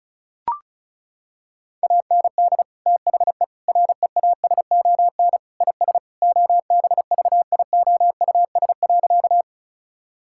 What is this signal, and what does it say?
Morse code, heard as a keyed tone, about 36 wpm, 700 Hz. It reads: AND THE REASON IS OBVIOUS.
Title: Sentences from Top 1600 Words 35wpm